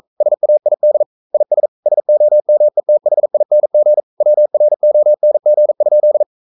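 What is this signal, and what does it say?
Morse code, heard as a keyed tone, about 42 wpm, 600 Hz. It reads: SAID IS SOMETHING WRONG?
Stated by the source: Morse code practice at 41 wpm, from CW med Gnister.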